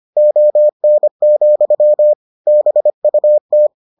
Morse code, heard as a keyed tone, about 25 wpm, 600 Hz. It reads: ON, BUT